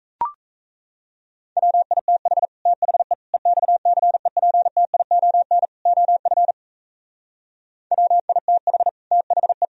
Morse code, heard as a keyed tone, about 34 wpm, 700 Hz. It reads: WITH THE EXCEPTION OF WITH THE
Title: Speed-Racing - Sets of 4 Words 28wpm